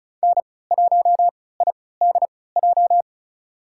Morse code, heard as a keyed tone, about 34 wpm, spaced as for 20 wpm, 700 Hz. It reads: N1IDJ
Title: USA Call Signs - 5 Characters - 3x Character Spacing - 35wpm